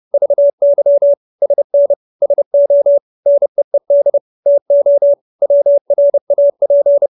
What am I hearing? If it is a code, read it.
Morse code, heard as a keyed tone, about 30 wpm, 570 Hz. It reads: VY SN SO NEED TO WRAP
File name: QSO321 30